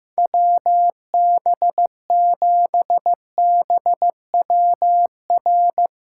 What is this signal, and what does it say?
Morse code, heard as a keyed tone, about 15 wpm, 700 Hz. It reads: WB7BWR
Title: USA Callsign DE Callsign 15wpm